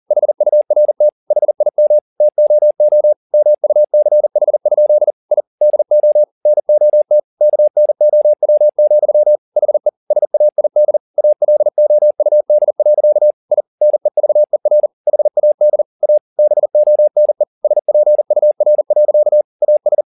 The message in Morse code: HURT HIM TOO MUCH? I DO NOT KNOW, HE SAID ALOUD. I NEVER HAD A BONE SPUR. AS

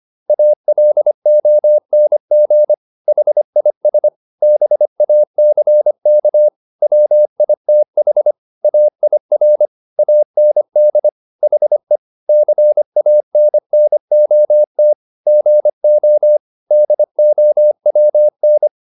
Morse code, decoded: ALONG HIS BACK WITH AIR AND HE CANNOT GO DOWN

